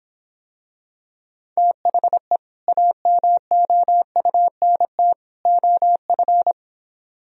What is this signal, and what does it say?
Morse code, heard as a keyed tone, about 26 wpm, 700 Hz. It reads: THE AMOUNT OF